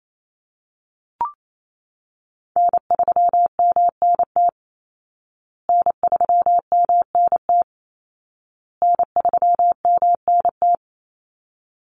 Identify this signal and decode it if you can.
Morse code, heard as a keyed tone, about 28 wpm, 700 Hz. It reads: N3MNT N3MNT N3MNT
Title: Call Signs Repeated 2x - 28wpm